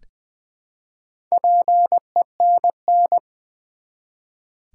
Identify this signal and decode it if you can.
Morse code, heard as a keyed tone, about 20 wpm, 700 Hz. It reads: PENN
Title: Build A Word - 1 Letter At A Time - 5-Letter Words - 20wpm